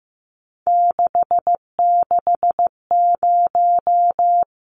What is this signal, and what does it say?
Morse code, heard as a keyed tone, about 15 wpm, 700 Hz. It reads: 660
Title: Numbers - 3-Digit 15wpm